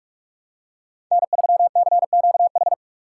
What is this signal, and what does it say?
Morse code, heard as a keyed tone, about 46 wpm, 690 Hz. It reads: N3CQH
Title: USA Call Signs - 5 Characters - 2x Character Spacing - 45wpm